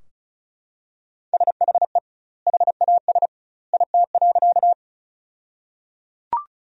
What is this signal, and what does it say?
Morse code, heard as a keyed tone, about 34 wpm, 700 Hz. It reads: SHE HAS IT.
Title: 2x Word-Spacing - Sentences from Top 100 Words - 35wpm